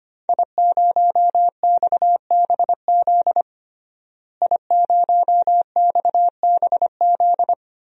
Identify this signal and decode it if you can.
Morse code, heard as a keyed tone, about 25 wpm, 700 Hz. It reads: I0XBZ I0XBZ
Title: DXCC-100 Call Signs - 5 Characters - Repeated 2x - 25wpm